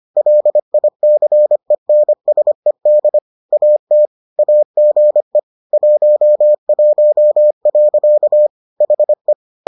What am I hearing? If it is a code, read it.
Morse code, heard as a keyed tone, about 25 wpm, 600 Hz. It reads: LICENSED AT AGE 11. HE